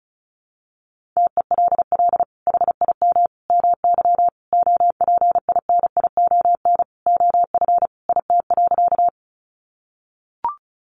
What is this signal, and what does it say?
Morse code, heard as a keyed tone, about 34 wpm, 700 Hz. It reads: TELL HIM MY OPINION OF IT.
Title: Sentences from Top 1200 Words 35wpm